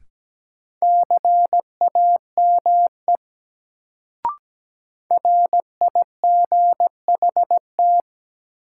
Morse code, heard as a keyed tone, about 17 wpm, 700 Hz. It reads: CAME RIGHT
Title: Sentences from Top 200 Words - New Word Drill - 17wpm